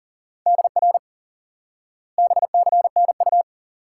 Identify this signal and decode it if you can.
Morse code, heard as a keyed tone, about 40 wpm, 700 Hz. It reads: DR BCNU
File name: IQE Course 3 - Lesson 89 - YAGI - Two Elements - 40wpm